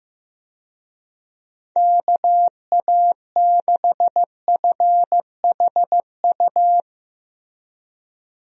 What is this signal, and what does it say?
Morse code, heard as a keyed tone, about 15 wpm, 700 Hz. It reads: KA6FHU
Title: USA Call Signs - 6 Characters - Repeated 3x - 15wpm